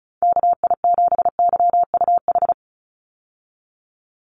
Morse code, heard as a keyed tone, about 34 wpm, 700 Hz. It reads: KI7YUH
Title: USA Call Signs 35wpm